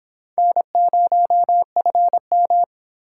Morse code, heard as a keyed tone, about 26 wpm, 700 Hz. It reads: N0FM